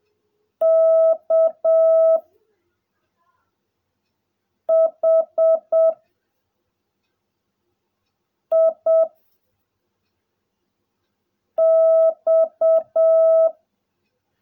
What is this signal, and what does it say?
Morse code, heard as a keyed tone, about 7 wpm, 660 Hz. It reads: K H I X
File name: jh8cbh / 2021cw / cw16a